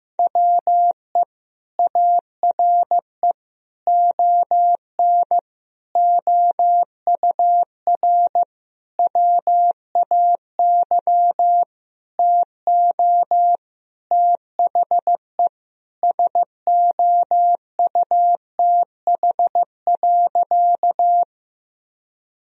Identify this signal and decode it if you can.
Morse code, heard as a keyed tone, about 16 wpm, 700 Hz. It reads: WE ARE ON OUR WAY TO THE SOUTH.
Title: Sentences from Top 1000 Words - Review Part 2 - 15wpm